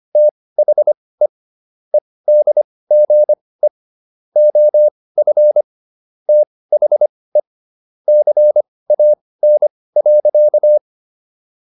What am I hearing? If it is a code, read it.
Morse code, heard as a keyed tone, about 25 wpm, spaced as for 18 wpm, 600 Hz. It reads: THE EDGE OF THE CAN.